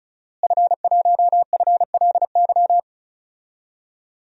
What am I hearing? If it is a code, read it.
Morse code, heard as a keyed tone, about 34 wpm, 700 Hz. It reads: F1FLY
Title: DXCC-100 Call Signs - 5 Characters - Repeated 3x - 35wpm